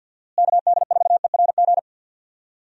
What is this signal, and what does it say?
Morse code, heard as a keyed tone, about 49 wpm, 700 Hz. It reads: KD4ERG
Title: USA Call Signs - 6 Characters - Repeated 2x - 50wpm